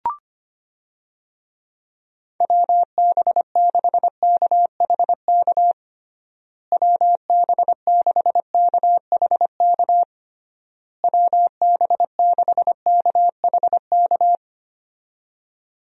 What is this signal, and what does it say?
Morse code, heard as a keyed tone, about 25 wpm, 700 Hz. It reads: WB6KHK WB6KHK WB6KHK
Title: USA Call Signs - 6 Characters - Repeated 3x - 25wpm